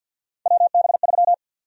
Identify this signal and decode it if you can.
Morse code, heard as a keyed tone, about 52 wpm, 700 Hz. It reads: WB3
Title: USA Call Sign Prefixes - Repeated 2x - 50wpm